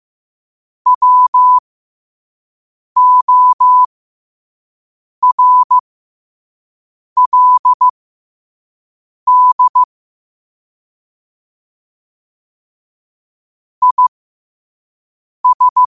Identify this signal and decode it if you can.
Morse code, heard as a keyed tone, about 15 wpm, spaced as for 5 wpm, 1000 Hz. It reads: WORLD IS
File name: qotdfiles / 5wpm